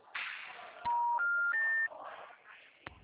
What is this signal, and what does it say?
Special information tone, with noise.